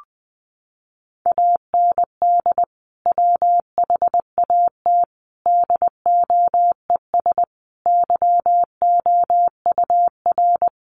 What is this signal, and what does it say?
Morse code, heard as a keyed tone, about 20 wpm, 700 Hz. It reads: AND WHAT DOES YOUR
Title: Sentences from Top 1200 Words 20wpm